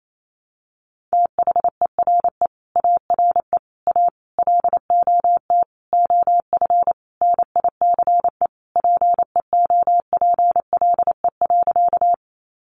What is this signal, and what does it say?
Morse code, heard as a keyed tone, about 28 wpm, 700 Hz. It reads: THERE ARE A LOT OF NICE PEOPLE.
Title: Sentences from Top 1100 Words 28wpm